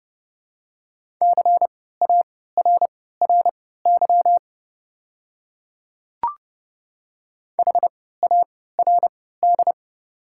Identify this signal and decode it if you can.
Morse code, heard as a keyed tone, about 30 wpm, spaced as for 17 wpm, 700 Hz. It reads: CARRY HARD
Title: 3x Character Spacing - Top 300 Words - 30wpm